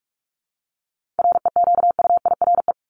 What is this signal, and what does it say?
Morse code, heard as a keyed tone, about 44 wpm, 700 Hz. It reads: REQUIRE
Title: ED Suffix - Top 100 Words with Base Words 45wpm